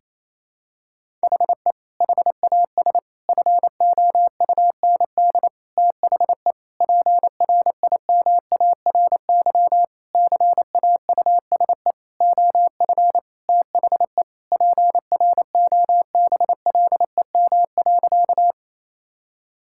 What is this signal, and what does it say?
Morse code, heard as a keyed tone, about 28 wpm, 700 Hz. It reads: HE HAS FOUND THE PRIMARY CAUSE OF THE PROBLEM.
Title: Sentences from Top 1000 Words - Review Part 5 - 28wpm